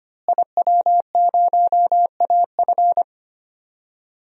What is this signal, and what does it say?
Morse code, heard as a keyed tone, about 25 wpm, 700 Hz. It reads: IW0AF
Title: DXCC-100 Call Signs - 5 Characters - Repeated 2x - 25wpm